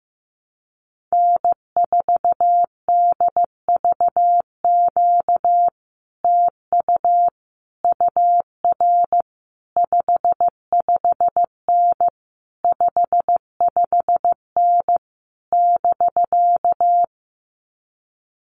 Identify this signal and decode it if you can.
Morse code, heard as a keyed tone, about 15 wpm, 700 Hz. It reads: N4DVQ TU UR 55N 55N <BK>
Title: Common QSO Phrases Encore 15wpm